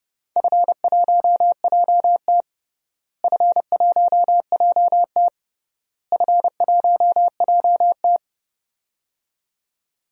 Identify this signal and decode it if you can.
Morse code, heard as a keyed tone, about 31 wpm, 700 Hz. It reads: F1JT F1JT F1JT
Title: DXCC-100 Call Signs - 4 Characters - Repeated 3x - 30wpm